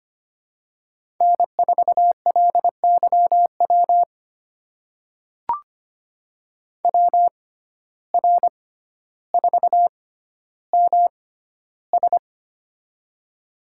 Morse code, heard as a keyed tone, about 25 wpm, spaced as for 13 wpm, 700 Hz. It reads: N4LYW WR4MS
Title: USA Call Signs - 6x Character Spacing - 25wpm